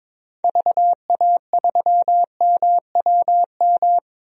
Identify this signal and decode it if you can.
Morse code, heard as a keyed tone, about 21 wpm, 700 Hz. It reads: VA3MWM
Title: DXCC-100 Call Signs - 6 Characters - 22wpm